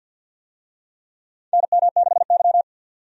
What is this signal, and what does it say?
Morse code, heard as a keyed tone, about 52 wpm, 700 Hz. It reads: NM6Y
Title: USA Call Signs - 4 Characters - 50wpm